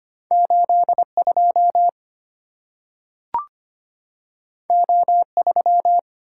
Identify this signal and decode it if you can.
Morse code, heard as a keyed tone, about 25 wpm, 700 Hz. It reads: 82 O3
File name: Two Letter-Number Combinations - 25wpm